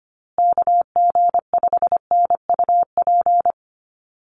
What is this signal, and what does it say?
Morse code, heard as a keyed tone, about 25 wpm, 700 Hz. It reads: KG5NUP
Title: USA Call Signs Repeated 2x 25wpm